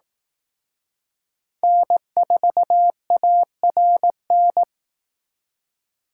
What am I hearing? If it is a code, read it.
Morse code, heard as a keyed tone, about 19 wpm, 700 Hz. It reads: N4ARN